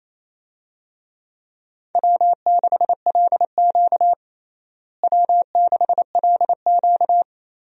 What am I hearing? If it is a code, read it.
Morse code, heard as a keyed tone, about 28 wpm, 700 Hz. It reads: W6LQ W6LQ
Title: USA Call Signs - 4 Characters - Repeated 3x - 28wpm